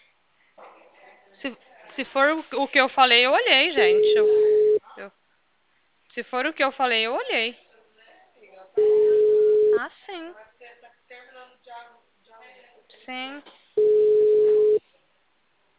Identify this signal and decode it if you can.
Ringback tone (Europe, 425 Hz), with speech.
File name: out- -2004- - - .13